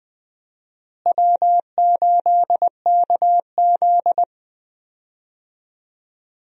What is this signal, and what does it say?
Morse code, heard as a keyed tone, about 20 wpm, 700 Hz. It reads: W8KZ
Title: USA Call Signs - 4 Characters - Repeated 2x - 20wpm